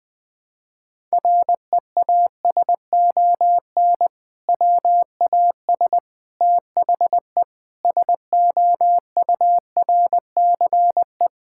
Morse code, heard as a keyed tone, about 20 wpm, 700 Hz. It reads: REASON WAS THE SOURCE
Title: Sentences from Top 1000 Words - Review Part 5 - 20wpm